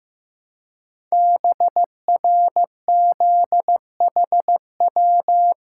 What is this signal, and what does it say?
Morse code, heard as a keyed tone, about 15 wpm, 700 Hz. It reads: BRZHW
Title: Code Groups - 5 Letters - 15wpm